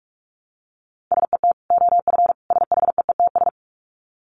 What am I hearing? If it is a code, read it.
Morse code, heard as a keyed tone, about 44 wpm, 700 Hz. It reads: SET OF SHEETS
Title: Dit Challenge 45wpm